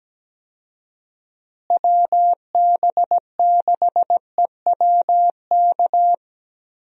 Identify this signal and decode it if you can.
Morse code, heard as a keyed tone, about 17 wpm, 700 Hz. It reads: WB6EWK
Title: USA Call Signs - 6 Characters - Repeated 3x - 17wpm